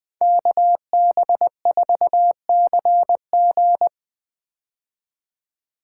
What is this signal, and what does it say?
Morse code, heard as a keyed tone, about 20 wpm, 700 Hz. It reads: KB4CG
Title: K1USN SST QSO Practice - 20wpm